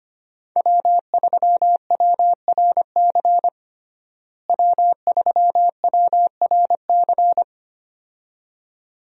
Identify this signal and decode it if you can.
Morse code, heard as a keyed tone, about 25 wpm, 700 Hz. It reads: W3WRC W3WRC
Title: USA Call Signs - 5 Characters - Repeated 2x - 25wpm